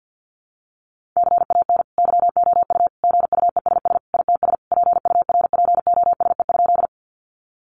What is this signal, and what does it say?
Morse code, heard as a keyed tone, about 50 wpm, 700 Hz. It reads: CAN YOU GUESS ITS PURPOSE?
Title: Sentences from Top 1500 Words 50wpm